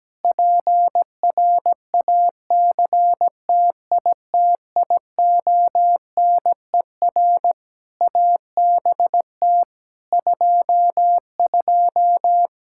Morse code, heard as a keyed tone, about 17 wpm, 700 Hz. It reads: PRACTITIONER ABT 22
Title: Practice QSO - Exchange 3 - More Info - 17wpm